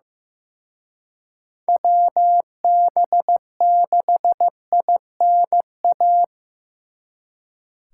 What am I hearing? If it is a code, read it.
Morse code, heard as a keyed tone, about 18 wpm, 700 Hz. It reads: WB6INA
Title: Speed Racing - USA 2x3 Call Signs 15wpm